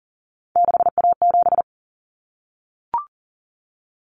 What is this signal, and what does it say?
Morse code, heard as a keyed tone, about 40 wpm, 700 Hz. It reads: 6A7
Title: Common Cut Numbers - 3-Digit 40wpm